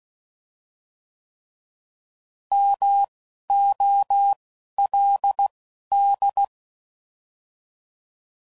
Morse code, heard as a keyed tone, about 16 wpm, spaced as for 11 wpm, 800 Hz. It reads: MOLD